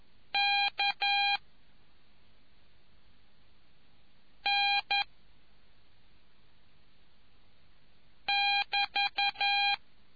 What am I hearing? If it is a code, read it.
Morse code, heard as a keyed tone, about 11 wpm, 2400 Hz. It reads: K N <BT>